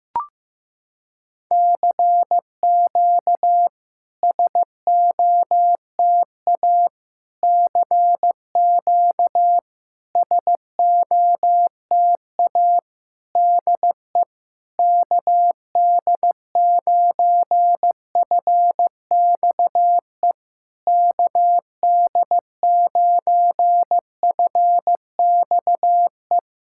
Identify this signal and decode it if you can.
Morse code, heard as a keyed tone, about 15 wpm, 700 Hz. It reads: CQ SOTA CQ SOTA DE KD9FXE KD9FXE
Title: SOTA QSO Practice 15wpm